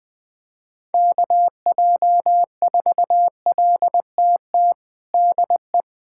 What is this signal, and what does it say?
Morse code, heard as a keyed tone, about 20 wpm, 700 Hz. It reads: KJ4LTT DE
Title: USA Callsign DE Callsign 20wpm